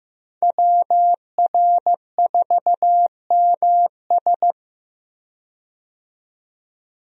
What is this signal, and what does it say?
Morse code, heard as a keyed tone, about 15 wpm, 700 Hz. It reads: WR4MS
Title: USA Call Signs - 5 Characters - 15wpm